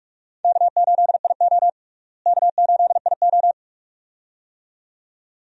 Morse code, heard as a keyed tone, about 41 wpm, 700 Hz. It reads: K8IO K8IO
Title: USA Call Signs - 4 Characters - Repeated 2x - 45wpm